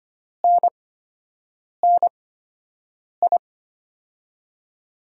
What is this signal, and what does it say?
Morse code, heard as a keyed tone, about 25 wpm, spaced as for 7 wpm, 700 Hz. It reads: NNI
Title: USA Call Signs - 8x Character Spacing - 25wpm